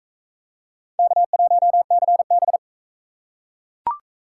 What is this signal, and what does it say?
Morse code, heard as a keyed tone, about 37 wpm, 700 Hz. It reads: K1CB